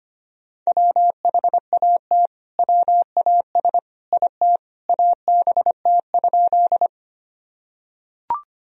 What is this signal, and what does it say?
Morse code, heard as a keyed tone, about 25 wpm, 700 Hz. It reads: WHAT WAS IT ABT?
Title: Sentences from Top 100 Words and Abbreviations 25wpm